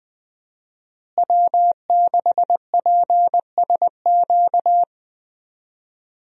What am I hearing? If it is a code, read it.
Morse code, heard as a keyed tone, about 20 wpm, 700 Hz. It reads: W6PSQ
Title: USA Call Signs - 5 Characters - Repeated 3x - 20wpm